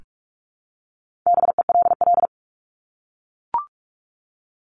Morse code, heard as a keyed tone, about 44 wpm, 700 Hz. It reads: BELL
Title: Top 1751-2000 Words 45wpm